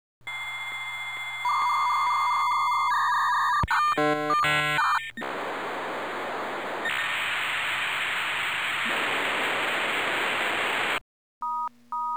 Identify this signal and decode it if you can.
Dial-up modem handshake.